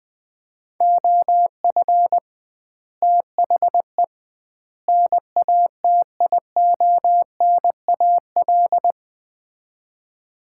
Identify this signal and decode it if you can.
Morse code, heard as a keyed tone, about 20 wpm, 700 Hz. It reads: OF THE NATIONAL
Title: 2x Word-Spacing - Sets of 3 Words - 20wpm